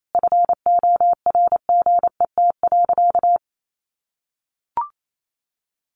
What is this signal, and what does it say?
Morse code, heard as a keyed tone, about 28 wpm, 700 Hz. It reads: FORGET.
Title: Sentences from Top 1500 Words 28wpm